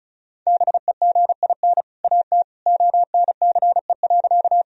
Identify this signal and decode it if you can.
Morse code, heard as a keyed tone, about 36 wpm, 700 Hz. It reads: BEGIN AT ONCE.
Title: Sentences from Top 1000 Words - Review Part 1 - 35wpm